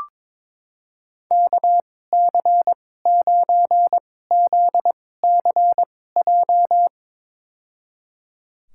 Morse code, heard as a keyed tone, about 22 wpm, spaced as for 16 wpm, 700 Hz. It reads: KC9ZCJ
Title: USA Call Signs - 6 Characters - 2x Character Spacing - 22wpm